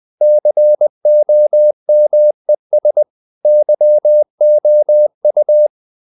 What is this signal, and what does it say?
Morse code, heard as a keyed tone, about 20 wpm, 600 Hz. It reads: COMES YOU
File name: the old man and the sea / the old man and the sea part 1 20wpm